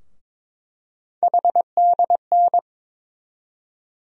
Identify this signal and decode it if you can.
Morse code, heard as a keyed tone, about 22 wpm, 700 Hz. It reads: HDN
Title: USA Call Sign Postfixes - 3 Letters 22wpm